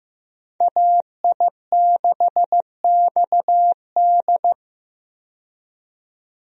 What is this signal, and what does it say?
Morse code, heard as a keyed tone, about 15 wpm, 700 Hz. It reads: AI6XD
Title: USA Call Signs - 5 Characters - 15wpm